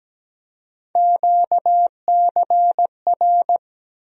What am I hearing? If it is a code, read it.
Morse code, heard as a keyed tone, about 17 wpm, 700 Hz. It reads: QCR
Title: USA Call Sign Postfixes - 3 Letters 17wpm